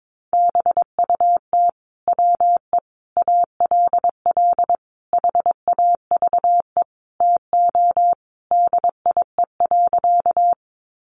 Morse code, heard as a keyed tone, about 22 wpm, 700 Hz. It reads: BUT WE ALL HAVE TO DIE.
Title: Sentences from Top 1300 Words 22wpm